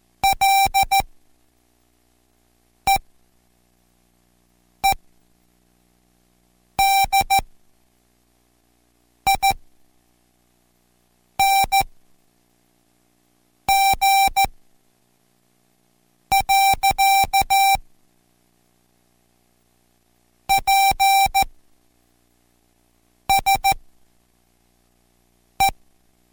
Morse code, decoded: LEEDING. PSE